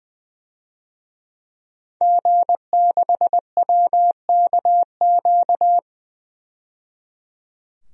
Morse code, heard as a keyed tone, about 21 wpm, 700 Hz. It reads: G6WKQ
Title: DXCC-100 Call Signs - 5 Characters - 20wpm